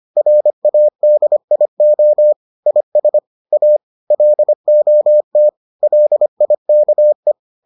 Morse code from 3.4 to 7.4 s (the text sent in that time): A LOT LIKE